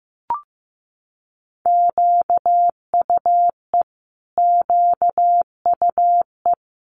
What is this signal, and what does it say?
Morse code, heard as a keyed tone, about 15 wpm, 700 Hz. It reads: QUE QUE
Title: Multi-letter Phonemes Repeated 3x 15wpm